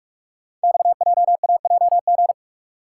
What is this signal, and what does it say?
Morse code, heard as a keyed tone, about 44 wpm, 690 Hz. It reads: XJAJG
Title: Code Groups - 5 Letters - 45wpm